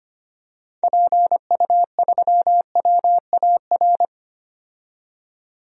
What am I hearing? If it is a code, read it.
Morse code, heard as a keyed tone, about 25 wpm, 690 Hz. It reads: PU3WAR
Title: DXCC-100 Call Signs - 6 Characters - 25wpm